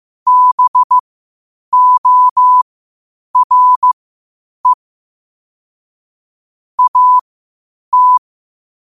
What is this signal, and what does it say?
Morse code, heard as a keyed tone, about 15 wpm, spaced as for 8 wpm, 1000 Hz. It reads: BORE AT
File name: qotdfiles / 8wpm